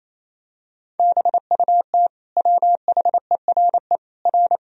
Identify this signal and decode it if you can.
Morse code, heard as a keyed tone, about 28 wpm, 700 Hz. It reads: BUT WHERE R
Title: Sentences from Top 200 Words and Abbreviations 28wpm